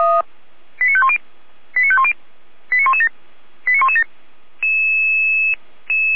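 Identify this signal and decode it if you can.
Five-tone selective call (ZVEI), heard as quick runs of five tones, tones of about 70 ms. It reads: 89510 89510 89117 89117
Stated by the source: ZVEI five-tone selective call received with an SDR.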